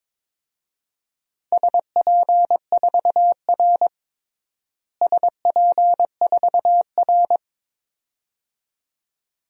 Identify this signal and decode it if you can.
Morse code, heard as a keyed tone, about 22 wpm, 700 Hz. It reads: SP4R SP4R
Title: DXCC-100 Call Signs - 4 Characters - Repeated 2x - 22wpm